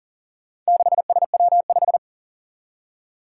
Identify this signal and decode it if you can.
Morse code, heard as a keyed tone, about 43 wpm, 700 Hz. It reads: 6SW5
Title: Four Challenging Letter-Number Combinations 40wpm